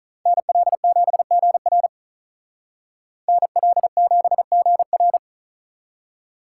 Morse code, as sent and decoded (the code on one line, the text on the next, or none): -. .-.. --... --. .-. / -. .-.. --... --. .-.
NL7GR NL7GR